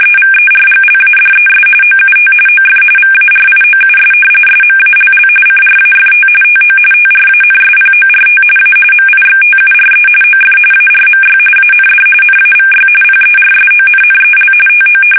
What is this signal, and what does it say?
RTTY, about 45.45 baud. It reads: EVERGLADES OUT OF VARIOUS SMALL CRAFT TO TRANSPORT A MIXTURE OF ARMY AND NAVY PERSONNEL TO
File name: fsk850 / usnavy2